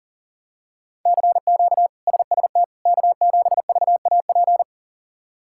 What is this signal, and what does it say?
Morse code, heard as a keyed tone, about 40 wpm, 700 Hz. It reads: CQ SST K7VAP
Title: K1USN SST QSO Practice - 40wpm